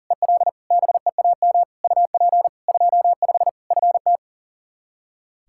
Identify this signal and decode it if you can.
Morse code, heard as a keyed tone, about 40 wpm, 700 Hz. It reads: EL BEAM UP 25 FT
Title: K1USN SST QSO Practice 40wpm